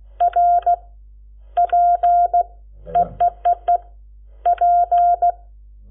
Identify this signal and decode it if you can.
Morse code, heard as a keyed tone, about 16 wpm, 700 Hz. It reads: R P EEEE P
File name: Lecciones / CLASE Nº7